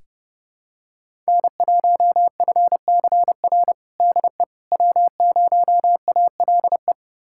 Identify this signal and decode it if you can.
Morse code, heard as a keyed tone, about 30 wpm, 700 Hz. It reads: N1FCR DE W0ALE
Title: USA Callsign DE Callsign - 5 Characters - 30wpm